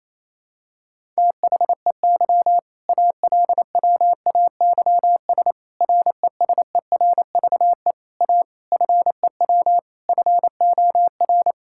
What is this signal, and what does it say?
Morse code, heard as a keyed tone, about 27 wpm, 690 Hz. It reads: THEY ALWAYS RESERVE A FEW FOR
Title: Sentences from Top 1000 Words - Review Part 6 - 28wpm